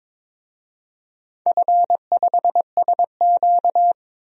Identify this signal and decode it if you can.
Morse code, heard as a keyed tone, about 22 wpm, 690 Hz. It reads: F5SQ
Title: DXCC-100 Call Signs - 4 Characters - 22wpm